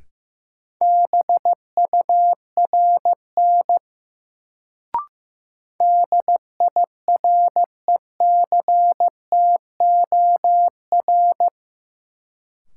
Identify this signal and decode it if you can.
Morse code, heard as a keyed tone, about 16 wpm, 700 Hz. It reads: BURN DIRECTOR
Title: Sentences from Top 1300 Words - New Word Drill - 15wpm